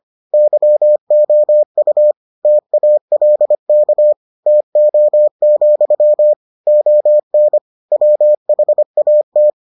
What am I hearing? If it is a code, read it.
Morse code, heard as a keyed tone, about 25 wpm, 600 Hz. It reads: YOU TALK TO, ON WHAT